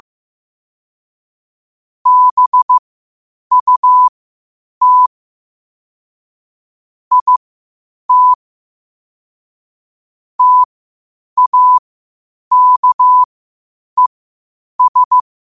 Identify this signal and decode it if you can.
Morse code, heard as a keyed tone, about 15 wpm, spaced as for 8 wpm, 1000 Hz. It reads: BUT IT TAKES
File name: qotdfiles / 8wpm